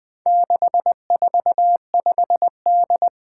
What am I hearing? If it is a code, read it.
Morse code, heard as a keyed tone, about 20 wpm, 700 Hz. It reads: 645D
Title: Four Challenging Letter-Number Combinations 20wpm